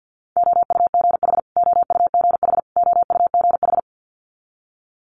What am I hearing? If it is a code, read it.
Morse code, heard as a keyed tone, about 50 wpm, 700 Hz. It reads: OUGH OUGH OUGH
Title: Multi-letter Phonemes Repeated 3x 50wpm